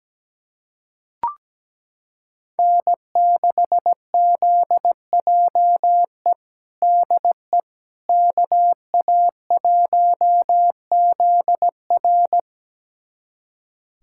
Morse code, heard as a keyed tone, about 17 wpm, 700 Hz. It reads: N6ZJE DE KA1ZR
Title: USA Callsign DE Callsign - 5 Characters - 17wpm